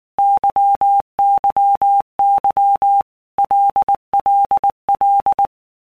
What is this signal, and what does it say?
Morse code, heard as a keyed tone, about 19 wpm, 800 Hz. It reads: YYY LLL
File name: Basic Letter Groups By Blocks / Groups@18 / LFCYQ@18